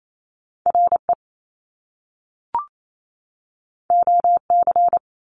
Morse code, heard as a keyed tone, about 28 wpm, 700 Hz. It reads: R E O C
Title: Top Two Letter Combinations in Words 28wpm